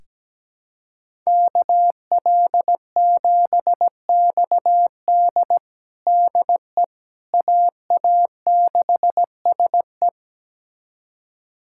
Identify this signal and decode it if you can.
Morse code, heard as a keyed tone, about 17 wpm, 700 Hz. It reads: KL7XD DE AA6SE
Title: USA Callsign DE Callsign - 5 Characters - 17wpm